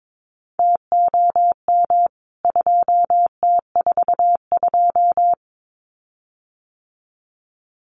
Morse code, heard as a keyed tone, about 22 wpm, 700 Hz. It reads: TOM 2T42
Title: CWT Exchange 22wpm